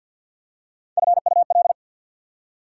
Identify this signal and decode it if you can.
Morse code, heard as a keyed tone, about 54 wpm, 690 Hz. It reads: FUL
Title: Prefixes and Suffixes 50wpm